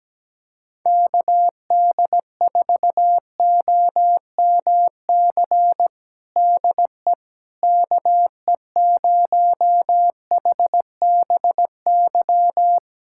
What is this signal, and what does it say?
Morse code, heard as a keyed tone, about 17 wpm, 700 Hz. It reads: KD4OMC DE KE0HBY
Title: USA Callsign DE Callsign - 6 Characters - 17wpm